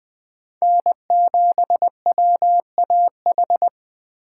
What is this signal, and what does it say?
Morse code, heard as a keyed tone, about 21 wpm, 700 Hz. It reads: N7WAH